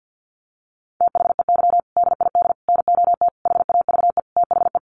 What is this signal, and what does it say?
Morse code, heard as a keyed tone, about 48 wpm, 700 Hz. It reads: THEY DID NOT HAVE THE
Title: Sentences from Top 1500 Words 50wpm